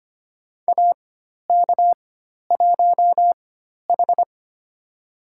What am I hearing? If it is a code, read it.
Morse code, heard as a keyed tone, about 25 wpm, spaced as for 12 wpm, 700 Hz. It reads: AK1H
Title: USA Call Signs - 4 Characters - 4x Character Spacing - 25wpm